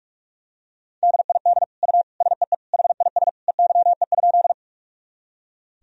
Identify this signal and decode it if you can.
Morse code, heard as a keyed tone, about 46 wpm, 700 Hz. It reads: DID U SEE HIS EYE?
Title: Sentences from Top 300 Words and Abbreviations 45wpm